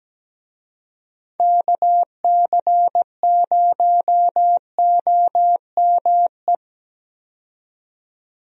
Morse code, heard as a keyed tone, about 17 wpm, 700 Hz. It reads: KC0OME
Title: USA Call Signs - 6 Characters - 17wpm